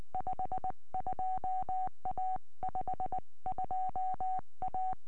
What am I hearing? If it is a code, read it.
Morse code, heard as a keyed tone, about 20 wpm, 750 Hz. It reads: 52A 52A